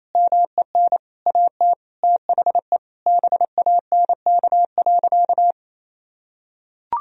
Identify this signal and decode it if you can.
Morse code, heard as a keyed tone, about 28 wpm, 700 Hz. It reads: MEN AT THE BANK.
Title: Sentences from Top 1000 Words - Review Part 4 - 28wpm